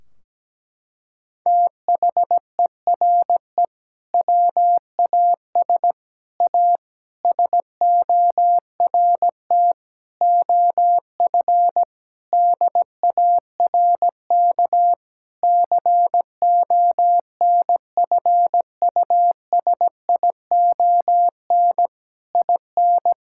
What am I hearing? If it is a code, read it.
Morse code, heard as a keyed tone, about 17 wpm, 700 Hz. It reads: THERE WAS A SORT OF DARK CONFUSION IN
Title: Sentences from Top 1600 Words 17wpm